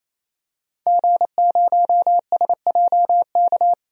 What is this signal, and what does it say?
Morse code, heard as a keyed tone, about 28 wpm, 700 Hz. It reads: G0SJK
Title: DXCC-100 Call Signs - 5 Characters - 28wpm